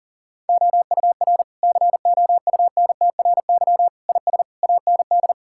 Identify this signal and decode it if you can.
Morse code, heard as a keyed tone, about 37 wpm, 700 Hz. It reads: OUR COUNTRY IS AND
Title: Sentences from Top 1000 Words - Review Part 6 - 40wpm